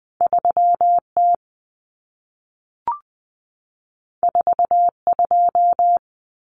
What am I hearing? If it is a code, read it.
Morse code, heard as a keyed tone, about 20 wpm, 700 Hz. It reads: 3T 42
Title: Common Cut Numbers - 2-Digit 20wpm